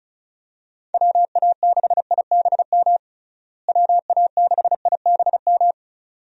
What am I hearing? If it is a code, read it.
Morse code, heard as a keyed tone, about 34 wpm, 700 Hz. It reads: WA6IBM WA6IBM
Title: USA Call Signs - 6 Characters - Repeated 2x - 35wpm